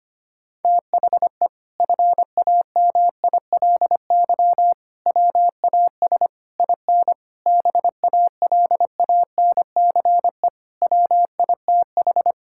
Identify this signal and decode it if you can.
Morse code, heard as a keyed tone, about 25 wpm, 700 Hz. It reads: THE FAMILY WAS IN BALANCE WITH
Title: Sentences from Top 1000 Words - Review Part 6 - 25wpm